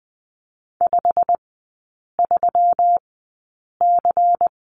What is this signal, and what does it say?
Morse code, heard as a keyed tone, about 20 wpm, spaced as for 8 wpm, 700 Hz. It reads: 53C
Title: 121 Lesson 20e - Number 3 - Three Characters 20wpm